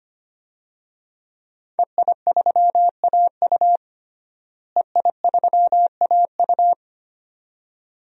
Morse code, heard as a keyed tone, about 25 wpm, 700 Hz. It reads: EI3AU EI3AU
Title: DXCC-100 Call Signs - 5 Characters - Repeated 2x - 25wpm